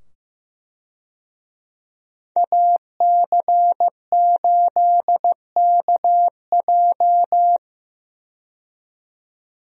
Morse code, heard as a keyed tone, about 15 wpm, 700 Hz. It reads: AC8KJ
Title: USA Call Signs - 5 Characters - 15wpm